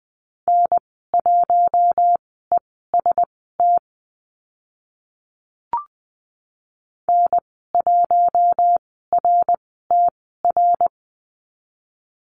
Morse code, heard as a keyed tone, about 20 wpm, spaced as for 14 wpm, 700 Hz. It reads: N1EST N1RTR
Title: 076 Lesson 13g - Letter L - Call Signs 20wpm